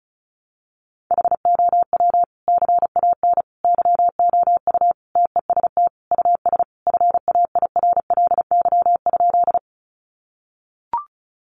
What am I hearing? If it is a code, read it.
Morse code, heard as a keyed tone, about 34 wpm, 700 Hz. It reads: HOW CAN YOU TEST US FAIRLY?
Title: Sentences from Top 1100 Words 35wpm